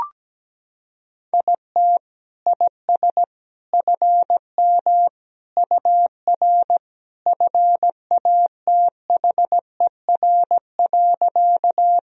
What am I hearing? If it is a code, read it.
Morse code, heard as a keyed tone, about 17 wpm, 700 Hz. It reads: IT IS FM UR FATHER.
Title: Sentences from Top 300 Words and Abbreviations 17wpm